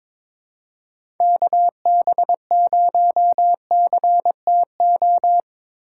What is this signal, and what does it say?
Morse code, heard as a keyed tone, about 21 wpm, 700 Hz. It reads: KB0CTO
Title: USA Call Signs - 6 Characters - 2x Character Spacing - 22wpm